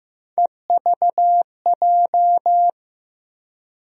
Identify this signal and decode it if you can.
Morse code, heard as a keyed tone, about 15 wpm, 700 Hz. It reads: EVJ
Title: USA Call Sign Postfixes - 3 Letters 15wpm